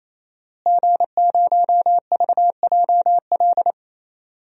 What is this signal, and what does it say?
Morse code, heard as a keyed tone, about 28 wpm, 700 Hz. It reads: G0VJL
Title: DXCC-100 Call Signs - 5 Characters - Repeated 3x - 28wpm